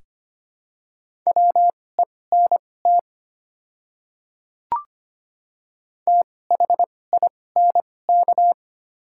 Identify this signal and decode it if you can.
Morse code, heard as a keyed tone, about 25 wpm, spaced as for 18 wpm, 700 Hz. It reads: WENT THINK
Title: 2x Character Spacing - Top 200 Words - 25wpm